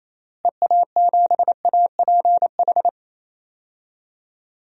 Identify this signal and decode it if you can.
Morse code, heard as a keyed tone, about 28 wpm, 700 Hz. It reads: EA7APH
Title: DXCC-100 Call Signs - 6 Characters - Repeated 3x - 28wpm